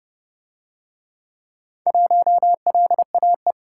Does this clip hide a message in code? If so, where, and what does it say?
Yes: Morse code from 1.9 to 3.5 s, heard as a keyed tone, about 27 wpm, 700 Hz. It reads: 1LAE